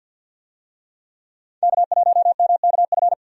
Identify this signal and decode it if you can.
Morse code, heard as a keyed tone, about 51 wpm, 700 Hz. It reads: K1MKF
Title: USA Call Signs - 5 Characters - 50wpm